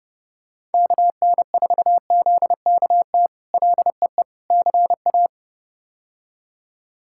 Morse code, heard as a keyed tone, about 30 wpm, 700 Hz. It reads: KN4ZKT LEE CA
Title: K1USN SST QSO Practice - 30wpm